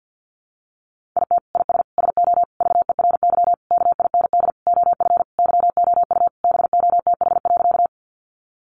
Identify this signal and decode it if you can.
Morse code, heard as a keyed tone, about 50 wpm, 700 Hz. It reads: IT IS SO VERY KIND OF YOU BOTH.
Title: Sentences from Top 300 Words 50wpm